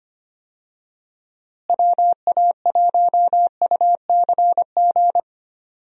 Morse code, heard as a keyed tone, about 25 wpm, 700 Hz. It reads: WA1UCG
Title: USA Call Signs - 6 Characters - Repeated 2x - 25wpm